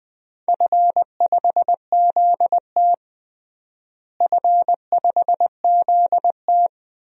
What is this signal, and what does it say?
Morse code, heard as a keyed tone, about 21 wpm, 700 Hz. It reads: F5ZT F5ZT
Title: DXCC-100 Call Signs - 4 Characters - Repeated 3x - 20wpm